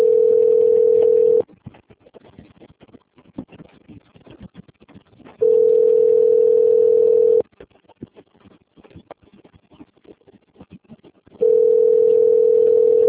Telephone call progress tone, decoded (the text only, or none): ringback tone